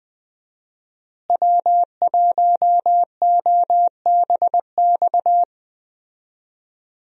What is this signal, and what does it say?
Morse code, heard as a keyed tone, about 20 wpm, 700 Hz. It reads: W1OBX
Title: USA Call Signs - 5 Characters - Repeated 3x - 20wpm